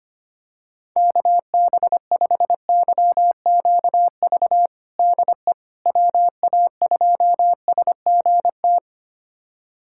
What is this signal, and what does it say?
Morse code, heard as a keyed tone, about 25 wpm, 700 Hz. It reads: KB5YQV DE WA2SGT
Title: USA Callsign DE Callsign 25wpm